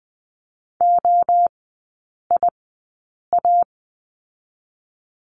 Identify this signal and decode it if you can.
Morse code, heard as a keyed tone, about 20 wpm, 700 Hz. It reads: O I A
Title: 029 Lesson 6e - Letter i - Three Characters 20wpm